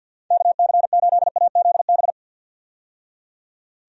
Morse code, heard as a keyed tone, about 51 wpm, 700 Hz. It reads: KC8AZB
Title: USA Call Signs - 6 Characters - 50wpm